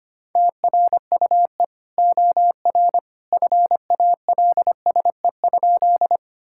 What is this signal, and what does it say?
Morse code, heard as a keyed tone, about 25 wpm, 700 Hz. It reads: TRUE OR FALSE?
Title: Sentences from Top 1600 Words 25wpm